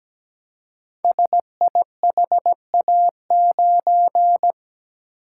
Morse code, heard as a keyed tone, about 17 wpm, 700 Hz. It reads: SIHA9
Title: Code Groups - 5 Letter:Numbers - 17wpm